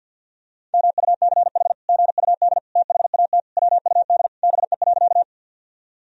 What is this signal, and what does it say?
Morse code, heard as a keyed tone, about 50 wpm, 700 Hz. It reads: MUCH GUD THAT WUD BE.
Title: Sentences from Top 200 Words and Abbreviations 50wpm